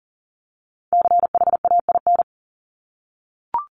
Morse code, heard as a keyed tone, about 40 wpm, 700 Hz. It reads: CHAIN
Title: Top 1251-1500 Words 40wpm